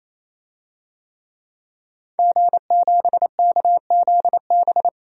Morse code, heard as a keyed tone, about 27 wpm, 700 Hz. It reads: G7KZB
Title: DXCC-100 Call Signs - 5 Characters - 28wpm